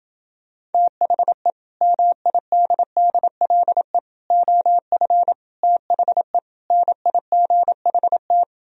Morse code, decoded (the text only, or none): THE MIDDLE OF THE NIGHT